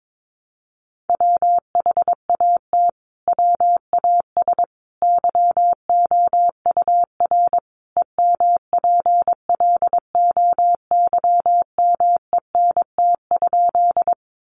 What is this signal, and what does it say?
Morse code, heard as a keyed tone, about 22 wpm, 700 Hz. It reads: WHAT WAS YOUR EMPLOYMENT?
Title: Sentences from Top 1500 Words 22wpm